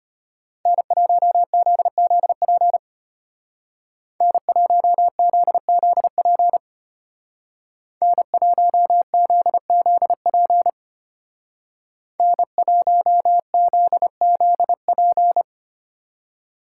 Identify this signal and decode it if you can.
Morse code, heard as a keyed tone, about 33 wpm, 700 Hz. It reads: N1ZZP N1ZZP N1ZZP N1ZZP